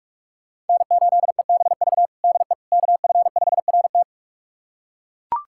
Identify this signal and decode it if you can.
Morse code, heard as a keyed tone, about 42 wpm, 700 Hz. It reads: N8EBV DE KF5RT
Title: USA Callsign DE Callsign - 5 Characters - 45wpm